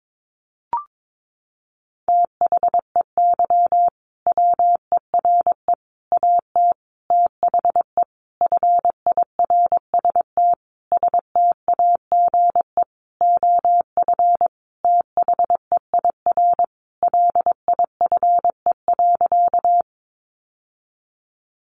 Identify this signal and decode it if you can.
Morse code, heard as a keyed tone, about 22 wpm, 700 Hz. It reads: THEY WERE AT THE FIRST STAGE OF THEIR LIFE.
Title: Sentences from Top 1200 Words 22wpm